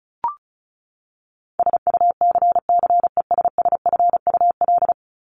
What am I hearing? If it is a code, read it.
Morse code, heard as a keyed tone, about 34 wpm, 700 Hz. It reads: SUCCESSFUL
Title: LY Suffix - Top 100 Words with Base Words 35wpm